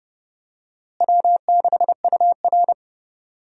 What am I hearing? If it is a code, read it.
Morse code, heard as a keyed tone, about 30 wpm, 700 Hz. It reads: W6UR
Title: USA Call Signs - 4 Characters - Repeated 3x - 30wpm